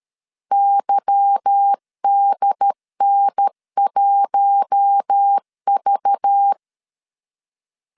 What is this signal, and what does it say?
Morse code, heard as a keyed tone, about 13 wpm, 800 Hz. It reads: YDN1V
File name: LEZIONI / 8 YJ45 / 8 YJ 12wpm 2235